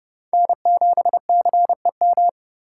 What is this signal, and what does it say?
Morse code, heard as a keyed tone, about 31 wpm, 700 Hz. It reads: N7CEM
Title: USA Call Signs - 5 Characters - 3x Character Spacing - 30wpm